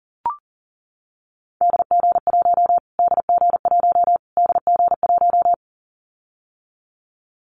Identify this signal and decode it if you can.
Morse code, heard as a keyed tone, about 40 wpm, 700 Hz. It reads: DG1 DG1 DG1
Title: Callsign Prefixes for Top 25 DXCC - Repeated 3x - 40wpm